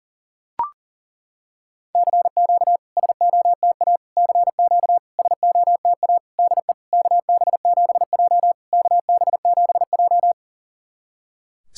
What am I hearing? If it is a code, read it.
Morse code, heard as a keyed tone, about 41 wpm, 690 Hz. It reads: CQ SOTA CQ SOTA DE KB7J KB7J
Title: SOTA QSO Practice 40wpm